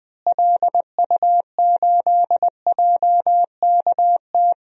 Morse code, heard as a keyed tone, about 20 wpm, 700 Hz. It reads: LU8JKT
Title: DXCC-100 Call Signs - 6 Characters - Repeated 3x - 20wpm